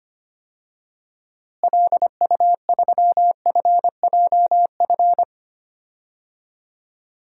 Morse code, heard as a keyed tone, about 25 wpm, 700 Hz. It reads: LU3FJF
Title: DXCC-100 Call Signs - 6 Characters - Repeated 2x - 25wpm